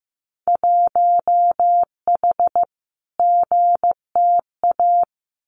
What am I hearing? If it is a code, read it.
Morse code, heard as a keyed tone, about 15 wpm, 700 Hz. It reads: 1H GTA
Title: Winter Field Day Exchanges 15wpm